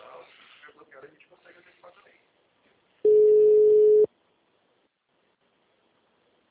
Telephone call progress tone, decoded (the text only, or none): ringback tone